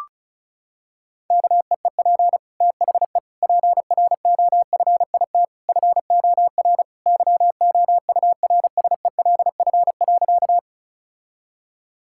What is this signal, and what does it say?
Morse code, heard as a keyed tone, about 34 wpm, 700 Hz. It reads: KEEP THE PROFIT FOR YOURSELF.
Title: Sentences from Top 1000 Words - Review Part 5 - 35wpm